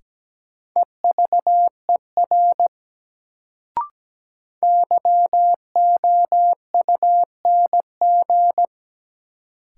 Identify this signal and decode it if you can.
Morse code, heard as a keyed tone, about 17 wpm, 700 Hz. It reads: EVER YOUNG
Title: Sentences from Top 400 Words - New Word Drill - 17wpm